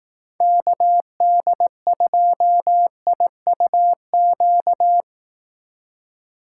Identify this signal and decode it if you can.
Morse code, heard as a keyed tone, about 18 wpm, 700 Hz. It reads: KD2IUQ
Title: Speed Racing - USA 2x3 Call Signs 15wpm